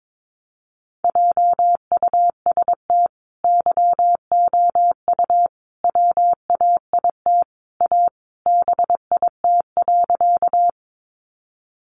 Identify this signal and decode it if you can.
Morse code, heard as a keyed tone, about 22 wpm, 700 Hz. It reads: JUST YOU WAIT A BIT.
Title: Sentences from Top 900 Words 22wpm